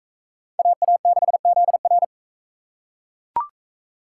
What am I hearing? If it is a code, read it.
Morse code, heard as a keyed tone, about 38 wpm, 700 Hz. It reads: AA6ZR